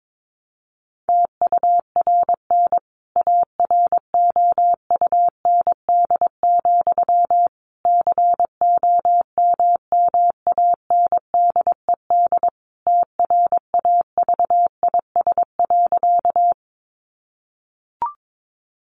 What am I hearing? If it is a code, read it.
Morse code, heard as a keyed tone, about 22 wpm, 700 Hz. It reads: TURN AROUND, COMMANDED TRAVIS.
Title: A Sound Of Thunder 22wpm